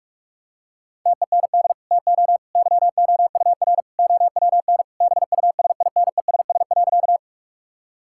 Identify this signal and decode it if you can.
Morse code, heard as a keyed tone, about 46 wpm, 700 Hz. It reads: TEND TO YOUR OWN BUSINESS.
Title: Sentences from Top 1000 Words - Review Part 5 - 45wpm